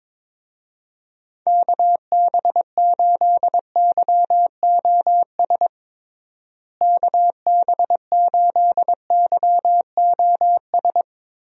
Morse code, heard as a keyed tone, about 22 wpm, 700 Hz. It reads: KB8YOS KB8YOS
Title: USA Call Signs - 6 Characters - Repeated 2x - 22wpm